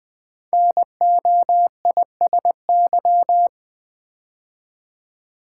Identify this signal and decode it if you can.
Morse code, heard as a keyed tone, about 20 wpm, 700 Hz. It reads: NOISY
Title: 5 Letter Words - First 3 Letters Given - 20wpm